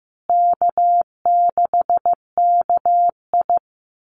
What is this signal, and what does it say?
Morse code, heard as a keyed tone, about 15 wpm, 700 Hz. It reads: K6KI
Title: Easy Call Signs Repeated 2x - 15wpm